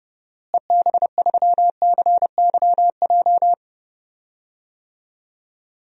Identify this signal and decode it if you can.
Morse code, heard as a keyed tone, about 30 wpm, 700 Hz. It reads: EB3CYJ
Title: DXCC-100 Call Signs - 6 Characters - Repeated 2x - 30wpm